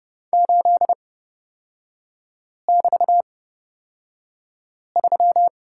Morse code, heard as a keyed tone, about 30 wpm, spaced as for 5 wpm, 700 Hz. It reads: 8<BT>3
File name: Lesson 38e - Letter q - Three Characters 30wpm